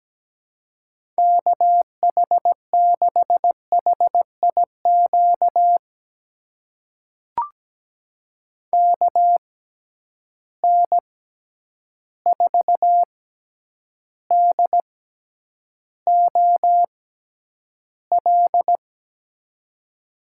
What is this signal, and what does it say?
Morse code, heard as a keyed tone, about 17 wpm, spaced as for 9 wpm, 700 Hz. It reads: KH6HIQ KN4DOL
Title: USA Call Signs - 6 Characters - 6x Character Spacing - 17wpm